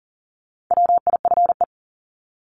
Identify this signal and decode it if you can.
Morse code, heard as a keyed tone, about 40 wpm, 700 Hz. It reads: WIFE
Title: Top 1751-2000 Words 40wpm